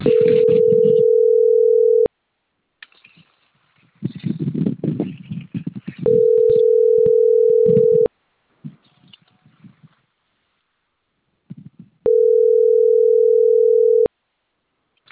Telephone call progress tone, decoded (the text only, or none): ringback tone